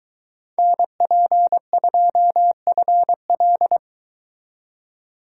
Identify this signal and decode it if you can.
Morse code, heard as a keyed tone, about 23 wpm, 700 Hz. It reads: NP2FL